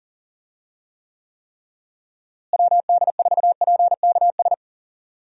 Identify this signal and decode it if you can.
Morse code, heard as a keyed tone, about 39 wpm, 700 Hz. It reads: WD4PKS
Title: USA Call Signs - 6 Characters - Repeated 3x - 40wpm